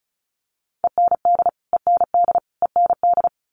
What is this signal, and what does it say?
Morse code, heard as a keyed tone, about 34 wpm, 700 Hz. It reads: END END END
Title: Top 500 Words Repeated 35wpm